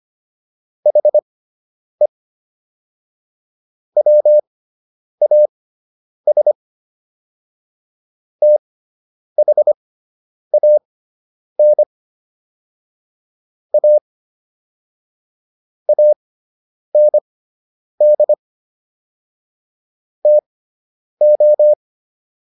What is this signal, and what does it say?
Morse code, heard as a keyed tone, about 25 wpm, spaced as for 9 wpm, 600 Hz. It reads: HE WAS THAN A AND TO